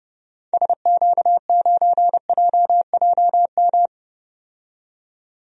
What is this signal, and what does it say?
Morse code, heard as a keyed tone, about 32 wpm, 700 Hz. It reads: SQ9JJM
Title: DXCC-100 Call Signs - 6 Characters - 30wpm